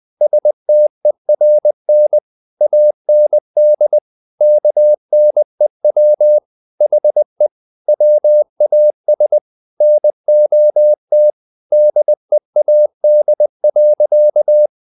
Morse code, heard as a keyed tone, about 20 wpm, 600 Hz. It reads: STERN AND KNEW HE WAS NOT DEAD.
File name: the old man and the sea part 2 20wpm